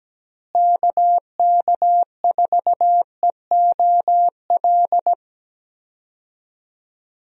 Morse code, heard as a keyed tone, about 17 wpm, 700 Hz. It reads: KK4EOL
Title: USA Call Signs - 6 Characters - Repeated 3x - 17wpm